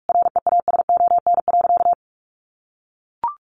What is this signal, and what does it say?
Morse code, heard as a keyed tone, about 44 wpm, 700 Hz. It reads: REASON.
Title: Sentences from Top 700 Words 45wpm